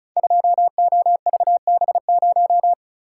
Morse code, heard as a keyed tone, about 36 wpm, 700 Hz. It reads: 2OVB0
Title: Code Groups - 5 Letter:Numbers - 35wpm